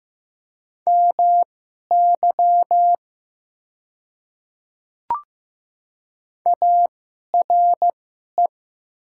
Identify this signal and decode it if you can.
Morse code, heard as a keyed tone, about 15 wpm, spaced as for 11 wpm, 700 Hz. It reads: MY ARE
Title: 2x Character Spacing - Top 500 Words - 15wpm